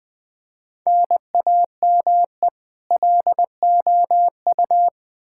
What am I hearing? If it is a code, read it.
Morse code, heard as a keyed tone, about 21 wpm, 700 Hz. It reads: NAME LOU
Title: Practice QSO - Exchange 1 - Beginning - 20wpm